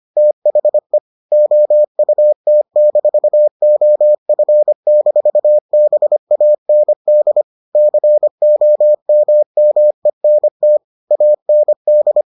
Morse code, heard as a keyed tone, about 25 wpm, 600 Hz. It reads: THE OUT-OF-BAND COMMENT AND